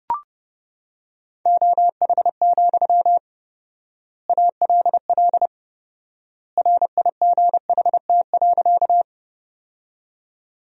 Morse code, heard as a keyed tone, about 30 wpm, 700 Hz. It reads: OH, ALL RIGHT.
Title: 4x Word-Spacing - Sentences from Top 500 Words - 30wpm